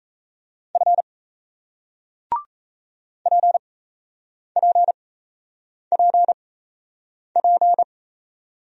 Morse code, heard as a keyed tone, about 38 wpm, spaced as for 8 wpm, 700 Hz. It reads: F PPPP